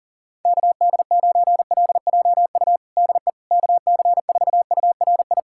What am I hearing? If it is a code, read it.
Morse code, heard as a keyed tone, about 40 wpm, 700 Hz. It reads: KD9LJU DE KC4URI
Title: USA Callsign DE Callsign 40wpm